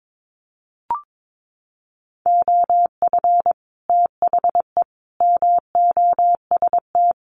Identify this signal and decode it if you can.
Morse code, heard as a keyed tone, about 22 wpm, 700 Hz. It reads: OF THE MOST
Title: Sets of 3 Words 22wpm